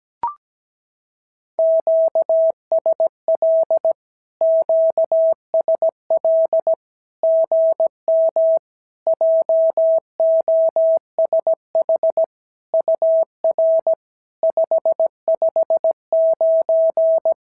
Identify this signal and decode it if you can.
Morse code, heard as a keyed tone, about 17 wpm, 650 Hz. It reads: QSL QSL GM JOSH UR 559
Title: SOTA QSO Practice 17wpm